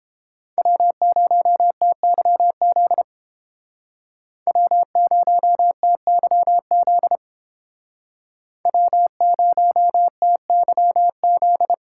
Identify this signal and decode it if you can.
Morse code, heard as a keyed tone, about 30 wpm, 700 Hz. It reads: W0TYZ W0TYZ W0TYZ